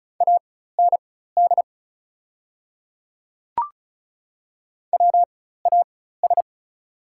Morse code, heard as a keyed tone, about 36 wpm, spaced as for 16 wpm, 690 Hz. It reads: AND WAS
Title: 4x Character Spacing - Top 100 Words - 35wpm